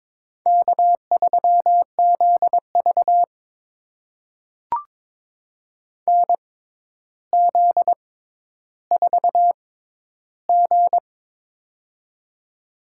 Morse code, heard as a keyed tone, about 22 wpm, spaced as for 11 wpm, 700 Hz. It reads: K3ZV NZ4G
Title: USA Call Signs - 4 Characters - 6x Character Spacing - 22wpm